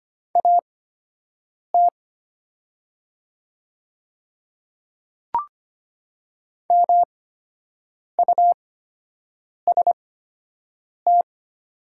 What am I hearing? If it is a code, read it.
Morse code, heard as a keyed tone, about 25 wpm, spaced as for 7 wpm, 700 Hz. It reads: AT MUST
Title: 8x Character Spacing - Top 300 Words - 25wpm